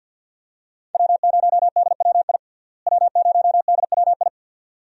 Morse code, heard as a keyed tone, about 49 wpm, 700 Hz. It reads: W0DWI W0DWI
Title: USA Call Signs - 5 Characters - Repeated 2x - 50wpm